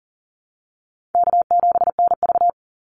Morse code, heard as a keyed tone, about 40 wpm, 700 Hz. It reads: K7NV
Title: Easy Call Signs Repeated 3x - 40wpm